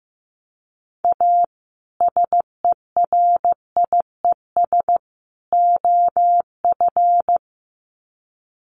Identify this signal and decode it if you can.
Morse code, heard as a keyed tone, about 15 wpm, 700 Hz. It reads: A SERIES OF
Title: Sets of 3 Words 15wpm